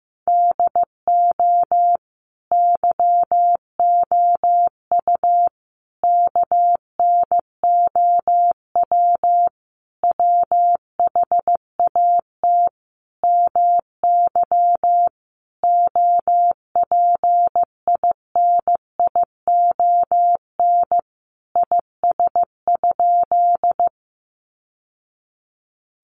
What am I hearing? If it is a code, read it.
Morse code, heard as a keyed tone, about 15 wpm, 700 Hz. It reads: DO YOU KNOW WHAT MY OPINION IS?
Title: Sentences from Top 1200 Words 15wpm